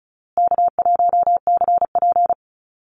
Morse code, heard as a keyed tone, about 34 wpm, 700 Hz. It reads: K1CP
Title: Call Signs Repeated 2x - 35wpm